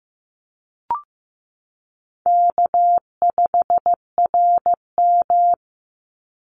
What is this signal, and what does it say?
Morse code, heard as a keyed tone, about 15 wpm, 700 Hz. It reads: K5RM
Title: Easy Call Signs - 15wpm